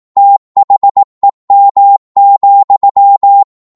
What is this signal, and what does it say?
Morse code, heard as a keyed tone, about 18 wpm, 800 Hz. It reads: THEM,
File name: 18wpm